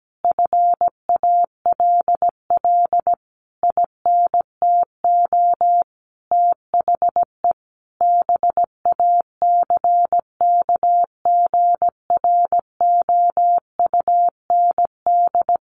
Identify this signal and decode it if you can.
Morse code, heard as a keyed tone, about 17 wpm, 700 Hz. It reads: FALL INTO THE BACKGROUND
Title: Sentences from Top 1300 Words 17wpm